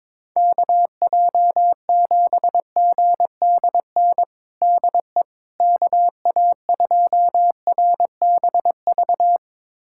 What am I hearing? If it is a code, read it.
Morse code, heard as a keyed tone, about 22 wpm, 700 Hz. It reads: KJ7GDN DE KA2RBV
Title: USA Callsign DE Callsign - 6 Characters - 22wpm